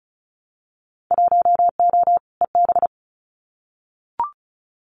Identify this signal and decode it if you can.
Morse code, heard as a keyed tone, about 34 wpm, 700 Hz. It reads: 1O EB
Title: Winter Field Day Exchanges 35wpm